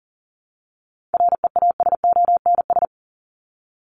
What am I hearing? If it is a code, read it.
Morse code, heard as a keyed tone, about 40 wpm, 700 Hz. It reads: REASONS
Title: ES:S Suffix - Top 100 Words with Base Words 40wpm